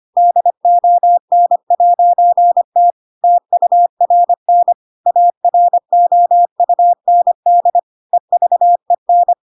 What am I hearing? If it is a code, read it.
Morse code, heard as a keyed tone, about 25 wpm, 700 Hz. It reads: DON'T TURN AROUND EVEN
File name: Pinocchio / Pinocchio ch4 25 25 ik1hge